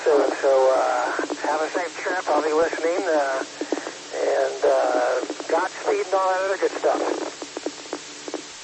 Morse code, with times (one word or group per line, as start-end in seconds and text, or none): none